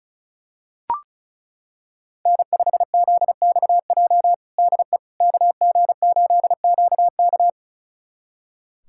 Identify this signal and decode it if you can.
Morse code, heard as a keyed tone, about 36 wpm, 700 Hz. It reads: N5ZXJ DE KG8QK
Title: USA Callsign DE Callsign - 5 Characters - 35wpm